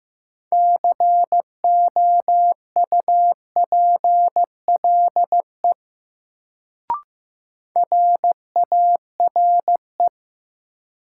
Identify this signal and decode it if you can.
Morse code, heard as a keyed tone, about 15 wpm, 700 Hz. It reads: COUPLE RARE
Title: Sentences from Top 1100 Words - New Word Drill Alt - 15wpm